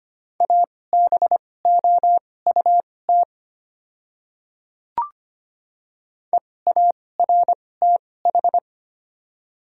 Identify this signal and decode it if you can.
Morse code, heard as a keyed tone, about 25 wpm, spaced as for 18 wpm, 700 Hz. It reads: ABOUT EARTH
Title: 2x Character Spacing - Top 200 Words - 25wpm